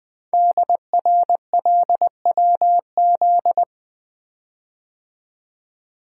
Morse code, heard as a keyed tone, about 20 wpm, 700 Hz. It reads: DRLWZ
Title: Code Groups - 5 Letters - 20wpm